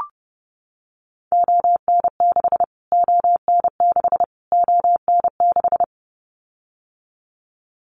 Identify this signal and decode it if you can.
Morse code, heard as a keyed tone, about 30 wpm, 700 Hz. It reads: ON6 ON6 ON6
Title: Callsign Prefixes for Top 25 DXCC - Repeated 3x - 30wpm